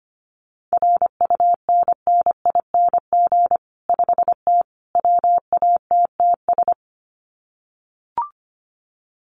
Morse code, heard as a keyed tone, about 25 wpm, 700 Hz. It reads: RUNNING 5T WATTS
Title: Common QSO Phrases 25wpm